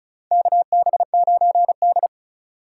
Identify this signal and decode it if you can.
Morse code, heard as a keyed tone, about 36 wpm, 700 Hz. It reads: KB9D
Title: USA Call Signs - 4x Character Spacing - 35wpm